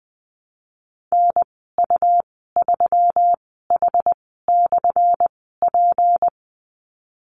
Morse code, heard as a keyed tone, about 20 wpm, spaced as for 15 wpm, 700 Hz. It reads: NU3H/P
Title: 179 Lesson 29g - Slash - Call Signs 20wpm